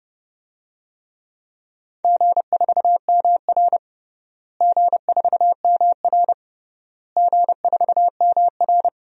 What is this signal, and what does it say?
Morse code, heard as a keyed tone, about 32 wpm, 700 Hz. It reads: G4MR G4MR G4MR
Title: DXCC-100 Call Signs - 4 Characters - Repeated 3x - 30wpm